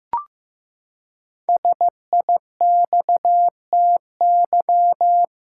Morse code, heard as a keyed tone, about 15 wpm, 700 Hz. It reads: SIXTY
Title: Pangram Challenge v2 - 15wpm